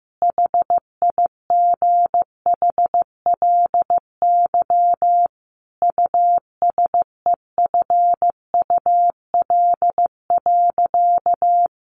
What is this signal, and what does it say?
Morse code, heard as a keyed tone, about 15 wpm, 700 Hz. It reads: HIGHLY USEFUL.
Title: Sentences from Top 1100 Words 15wpm